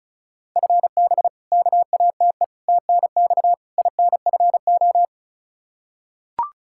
Morse code, heard as a keyed tone, about 34 wpm, 700 Hz. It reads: FB KATE TNX INFO
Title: Practice QSO - Exchange 3 - More Info - 35wpm